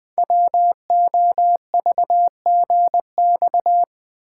Morse code, heard as a keyed tone, about 20 wpm, 700 Hz. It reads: WOVGX
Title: Code Groups - 5 Letters - 20wpm